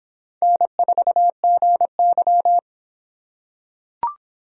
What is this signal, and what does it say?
Morse code, heard as a keyed tone, about 24 wpm, 700 Hz. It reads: N4GY